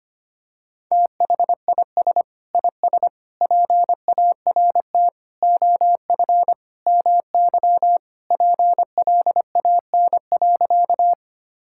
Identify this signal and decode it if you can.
Morse code, heard as a keyed tone, about 25 wpm, 700 Hz. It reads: THIS IS PART OF MY PLAN.
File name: practice / 25wpm / Sentences with 2-Word Phrases - 25wpm